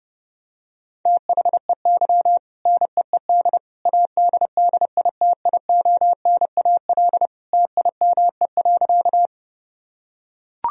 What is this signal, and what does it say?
Morse code, heard as a keyed tone, about 28 wpm, 700 Hz. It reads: THEY NEED ADDITIONAL TIME.
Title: Sentences from Top 1000 Words - Review Part 4 - 30wpm